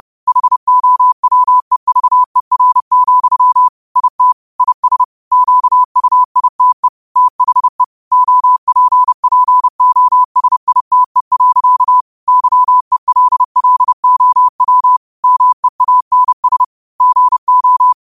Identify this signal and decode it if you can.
Morse code, heard as a keyed tone, about 30 wpm, 1000 Hz. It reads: HOWEVER, IT IS QUITE THE OPPOSITE. YELLOW MEANS GO